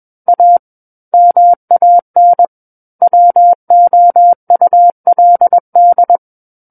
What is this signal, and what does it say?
Morse code, heard as a keyed tone, about 21 wpm, 700 Hz. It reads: A MAN WOULD
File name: Book One / 12-Chapter 12 - 21 WPM